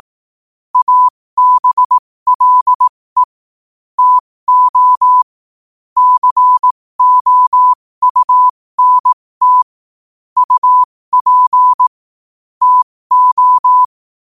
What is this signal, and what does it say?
Morse code, heard as a keyed tone, about 18 wpm, spaced as for 15 wpm, 1000 Hz. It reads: ABLE TO COUNT UP TO
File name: qotdfiles / 15wpm